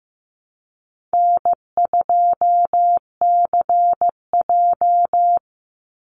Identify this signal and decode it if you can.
Morse code, heard as a keyed tone, about 15 wpm, 700 Hz. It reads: N2CJ
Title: Easy Call Signs - 15wpm